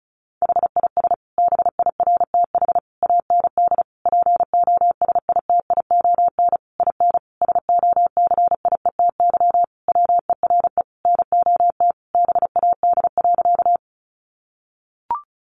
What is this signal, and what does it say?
Morse code, heard as a keyed tone, about 34 wpm, 700 Hz. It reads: HIS BIRTH AND POSITION IN SOCIETY WERE NOT BAD.
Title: Sentences from Top 900 Words 35wpm